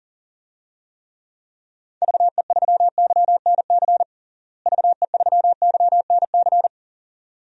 Morse code, heard as a keyed tone, about 40 wpm, 700 Hz. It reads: VE3YNC VE3YNC
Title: DXCC-100 Call Signs - 6 Characters - Repeated 2x - 40wpm